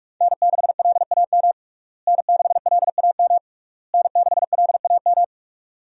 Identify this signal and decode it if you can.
Morse code, heard as a keyed tone, about 43 wpm, 700 Hz. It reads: N6LAM N6LAM N6LAM
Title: USA Call Signs - 5 Characters - Repeated 3x - 45wpm